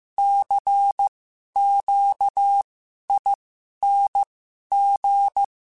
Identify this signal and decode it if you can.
Morse code, heard as a keyed tone, about 15 wpm, spaced as for 11 wpm, 800 Hz. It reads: CQING
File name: Special INGs-mp3s / Special INGs-mp3s / Sngl Ing @10